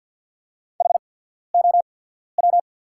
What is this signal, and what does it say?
Morse code, heard as a keyed tone, about 48 wpm, spaced as for 14 wpm, 690 Hz. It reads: HOW
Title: 8x Character Spacing - Top 200 Words - 50wpm